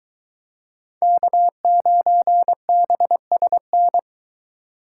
Morse code, heard as a keyed tone, about 23 wpm, 700 Hz. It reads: K9BSN